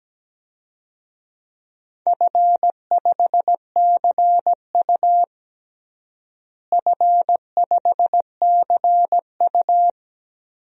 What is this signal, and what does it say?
Morse code, heard as a keyed tone, about 17 wpm, 700 Hz. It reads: F5CU F5CU
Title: DXCC-100 Call Signs - 4 Characters - Repeated 3x - 17wpm